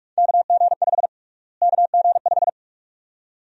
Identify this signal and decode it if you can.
Morse code, heard as a keyed tone, about 46 wpm, 700 Hz. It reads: KG5 KG5
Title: USA Call Sign Prefixes - Repeated 2x - 45wpm